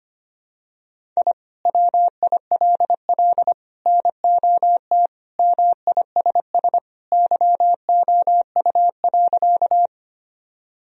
Morse code, heard as a keyed tone, about 25 wpm, 700 Hz. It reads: I WILL NOT MISS YOU.
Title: Sentences from Top 1000 Words - Review Part 3 - 25wpm